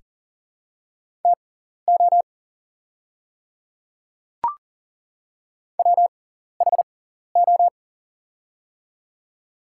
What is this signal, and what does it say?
Morse code, heard as a keyed tone, about 40 wpm, spaced as for 14 wpm, 700 Hz. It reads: TO WHO
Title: 6x Character Spacing - Top 300 Words - 40wpm